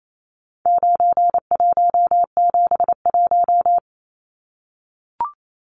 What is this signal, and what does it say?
Morse code, heard as a keyed tone, about 28 wpm, 700 Hz. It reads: 9171
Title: Numbers - 4-Digit 28wpm